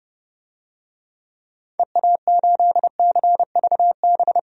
Morse code, heard as a keyed tone, about 30 wpm, 700 Hz. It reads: EA8CVB
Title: DXCC-100 Call Signs - 6 Characters - 30wpm